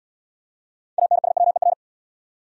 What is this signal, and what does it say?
Morse code, heard as a keyed tone, about 74 wpm, 700 Hz. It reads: ANSWER